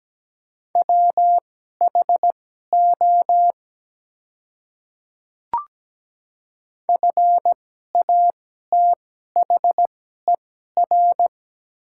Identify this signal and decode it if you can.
Morse code, heard as a keyed tone, about 17 wpm, spaced as for 12 wpm, 700 Hz. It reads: WHO FATHER
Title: 2x Character Spacing - Top 300 Words - 17wpm